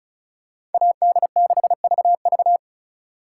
Morse code, heard as a keyed tone, about 36 wpm, 700 Hz. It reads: AD6VV
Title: USA Call Signs - 5 Characters - 2x Character Spacing - 35wpm